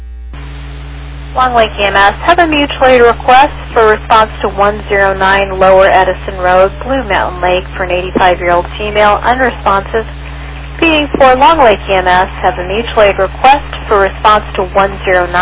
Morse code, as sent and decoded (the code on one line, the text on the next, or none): none